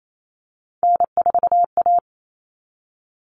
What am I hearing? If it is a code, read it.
Morse code, heard as a keyed tone, about 28 wpm, 700 Hz. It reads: N4A
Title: All Cut Numbers - 3-Digit 28wpm